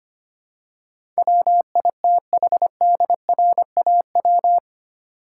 Morse code, heard as a keyed tone, about 25 wpm, 700 Hz. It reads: WITHDRAW
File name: 8 Letter Words - First 3 Letters Given - 25wpm